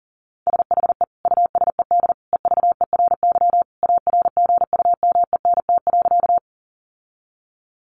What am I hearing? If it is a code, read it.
Morse code, heard as a keyed tone, about 40 wpm, 700 Hz. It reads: SHE USED EVERY ARGUMENT.
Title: Sentences from Top 1400 Words 40wpm